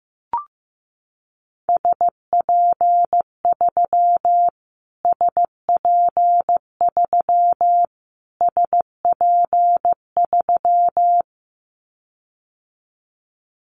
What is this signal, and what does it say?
Morse code, heard as a keyed tone, about 15 wpm, 700 Hz. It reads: SP3 SP3 SP3
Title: Callsign Prefixes for Top 25 DXCC - Repeated 3x - 15wpm